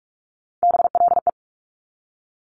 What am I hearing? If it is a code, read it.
Morse code, heard as a keyed tone, about 46 wpm, 700 Hz. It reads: BLE
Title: Top Three Letter Combinations in Words 45wpm